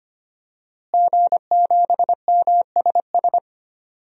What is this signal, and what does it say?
Morse code, heard as a keyed tone, about 25 wpm, 700 Hz. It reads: G7MSS
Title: DXCC-100 Call Signs - 5 Characters - Repeated 2x - 25wpm